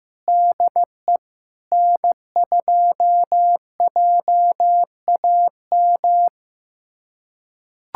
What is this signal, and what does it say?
Morse code, heard as a keyed tone, about 16 wpm, 700 Hz. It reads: DE N2JAM
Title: USA Callsign DE Callsign - 5 Characters - 15wpm